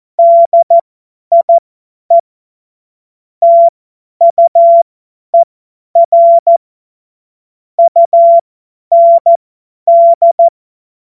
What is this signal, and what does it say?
Morse code, heard as a keyed tone, about 14 wpm, spaced as for 10 wpm, 680 Hz. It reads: DIE TUER UND
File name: Morsekurs / Witze / Witz 13 Februar 2024 50 F70